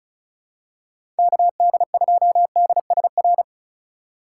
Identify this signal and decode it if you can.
Morse code, heard as a keyed tone, about 36 wpm, 700 Hz. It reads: KD2DSR
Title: USA Call Signs - 6 Characters - 2x Character Spacing - 35wpm